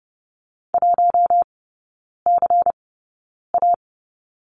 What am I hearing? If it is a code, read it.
Morse code, heard as a keyed tone, about 30 wpm, spaced as for 9 wpm, 700 Hz. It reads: 1CA
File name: Lesson 16e - Letter c - Three Characters 30wpm